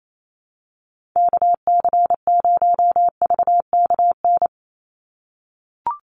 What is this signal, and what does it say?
Morse code, heard as a keyed tone, about 28 wpm, 700 Hz. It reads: KC0VKN
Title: Call Signs - 28wpm